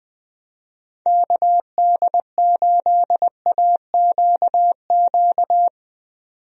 Morse code, heard as a keyed tone, about 20 wpm, 700 Hz. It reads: KD8AQQ
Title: USA Call Signs - 6 Characters - 20wpm